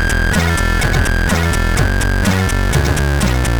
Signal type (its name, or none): none